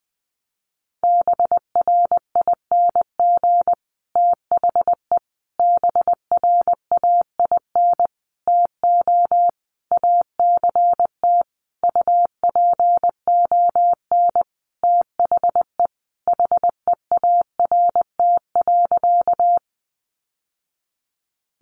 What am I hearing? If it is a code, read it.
Morse code, heard as a keyed tone, about 20 wpm, 700 Hz. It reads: BRING THE BRAIN TO ACT UPON THE HEART.
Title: Sentences from Top 1200 Words 20wpm